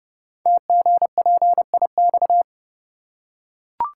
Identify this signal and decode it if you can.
Morse code, heard as a keyed tone, about 30 wpm, 690 Hz. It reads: TGPIX
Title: Code Groups - 5 Letters - 30wpm